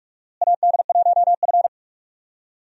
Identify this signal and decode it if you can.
Morse code, heard as a keyed tone, about 46 wpm, 700 Hz. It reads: AD1F
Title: USA Call Signs - 4 Characters - Repeated 3x - 45wpm